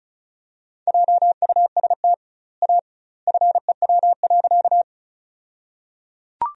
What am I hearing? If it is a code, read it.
Morse code, heard as a keyed tone, about 34 wpm, 700 Hz. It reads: JUST A FEW.
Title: 2x Word-Spacing - Sentences from Top 300 Words - 35wpm